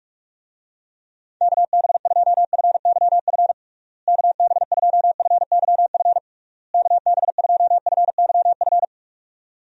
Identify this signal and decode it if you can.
Morse code, heard as a keyed tone, about 46 wpm, 700 Hz. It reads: KB2FYF KB2FYF KB2FYF
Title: USA Call Signs - 6 Characters - Repeated 3x - 45wpm